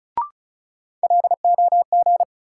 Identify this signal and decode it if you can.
Morse code, heard as a keyed tone, about 34 wpm, 700 Hz. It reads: LOG
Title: Sentences from Top 1400 Words - New Word Drill Alt - 35wpm